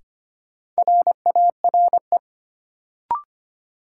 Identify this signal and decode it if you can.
Morse code, heard as a keyed tone, about 25 wpm, 700 Hz. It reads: RARE
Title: Sentences from Top 1100 Words - New Word Drill - 25wpm